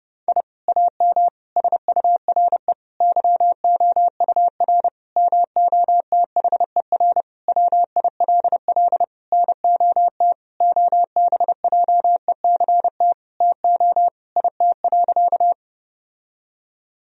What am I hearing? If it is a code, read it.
Morse code, heard as a keyed tone, about 31 wpm, 700 Hz. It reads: I AM SURE YOUR MOTHER WILL NOT OBJECT TO IT.
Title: Sentences from Top 1000 Words - Review Part 3 - 30wpm